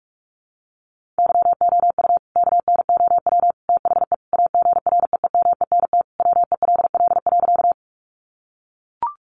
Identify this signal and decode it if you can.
Morse code, heard as a keyed tone, about 44 wpm, 700 Hz. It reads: YOU KNOW THE AGREEMENT WELL.
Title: Sentences from Top 1500 Words 45wpm